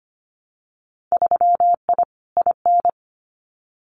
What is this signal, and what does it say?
Morse code, heard as a keyed tone, about 25 wpm, 700 Hz. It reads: 3I IN
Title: Winter Field Day Exchanges 25wpm